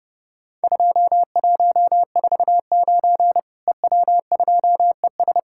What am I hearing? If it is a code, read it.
Morse code, heard as a keyed tone, about 30 wpm, 700 Hz. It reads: 2149 EW2ES
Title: Worked All Europe DX Contest - QTC Exchange - 30wpm